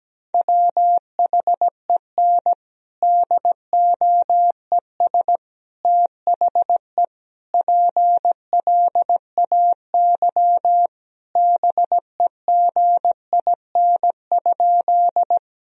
Morse code, heard as a keyed tone, about 17 wpm, 700 Hz. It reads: WHEN DOES THE PLAY BEGIN?
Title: Sentences from Top 1000 Words - Review Part 1 - 17wpm